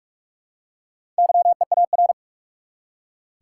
Morse code, heard as a keyed tone, about 44 wpm, 700 Hz. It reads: YEAR
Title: Four Letter Words - 45wpm